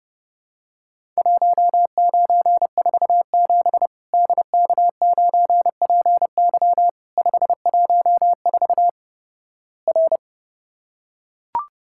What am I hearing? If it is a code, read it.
Morse code, heard as a keyed tone, about 30 wpm, 700 Hz. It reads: 1947 DK9PY 514 R
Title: Worked All Europe DX Contest - QTC Exchange - 30wpm